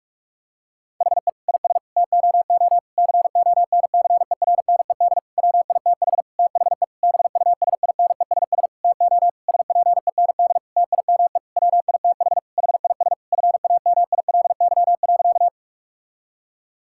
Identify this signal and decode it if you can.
Morse code, heard as a keyed tone, about 42 wpm, 700 Hz. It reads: HE IS TOO CONCERNED WITH THE BUSINESS TO SPEND TIME WITH HIS FAMILY.
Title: Sentences from Top 1000 Words - Review Part 6 - 45wpm